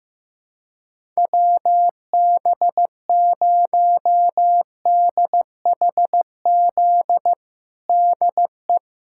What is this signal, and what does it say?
Morse code, heard as a keyed tone, about 15 wpm, 700 Hz. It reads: WB0DHZ DE
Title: USA Callsign DE Callsign - 6 Characters - 15wpm